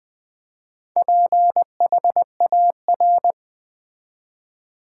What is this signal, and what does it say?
Morse code, heard as a keyed tone, about 20 wpm, 700 Hz. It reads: PHAR
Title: Build A Word - 2 Letters At A Time - 8-Letter Words - 20wpm